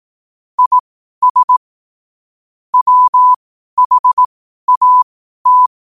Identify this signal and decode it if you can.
Morse code, heard as a keyed tone, about 18 wpm, spaced as for 12 wpm, 1000 Hz. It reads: IS WHAT
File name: qotdfiles / 12wpm